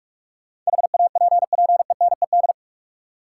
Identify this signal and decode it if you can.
Morse code, heard as a keyed tone, about 43 wpm, 700 Hz. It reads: HAPPENED
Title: Sentences from Top 1000 Words - Review Part 2 - New Word Drill - 45wpm